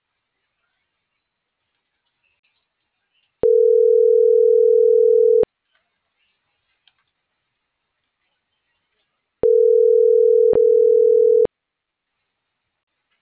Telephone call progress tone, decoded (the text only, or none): ringback tone